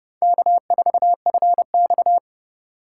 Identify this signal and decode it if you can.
Morse code, heard as a keyed tone, about 28 wpm, 700 Hz. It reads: K4FX